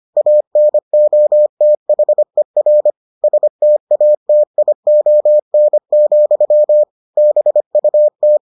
Morse code, heard as a keyed tone, about 25 wpm, 600 Hz. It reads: ANOTHER STATION, BUT